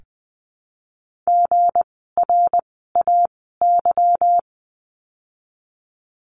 Morse code, heard as a keyed tone, about 20 wpm, 700 Hz. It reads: G R A Y
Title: 160 Lesson 26f - Letter g - Words 20wpm